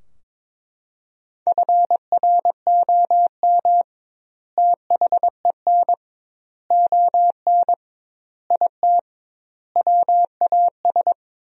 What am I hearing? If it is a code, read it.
Morse code, heard as a keyed tone, about 23 wpm, 700 Hz. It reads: FROM THEN ON IT WAS
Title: 2x Word-Spacing - Sentences from Top 400 Words - 22wpm